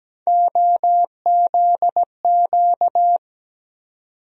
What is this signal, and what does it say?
Morse code, heard as a keyed tone, about 17 wpm, 700 Hz. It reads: OZQ
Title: USA Call Sign Postfixes - 3 Letters 17wpm